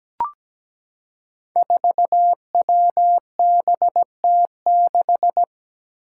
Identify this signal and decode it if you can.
Morse code, heard as a keyed tone, about 17 wpm, 700 Hz. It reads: 4WBT6
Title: Five Challenging Letter-Number Combinations 17wpm